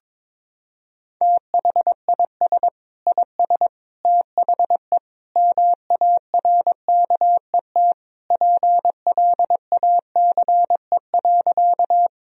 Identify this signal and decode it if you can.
Morse code, heard as a keyed tone, about 22 wpm, 700 Hz. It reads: THIS IS THE MARKET PLACE.
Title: Sentences from Top 1000 Words - Review Part 3 - 22wpm